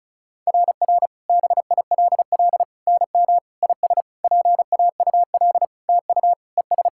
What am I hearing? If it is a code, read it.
Morse code, heard as a keyed tone, about 36 wpm, 690 Hz. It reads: RR BILL NM IS PAUL TU ES
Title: SOTA QSO Practice 35wpm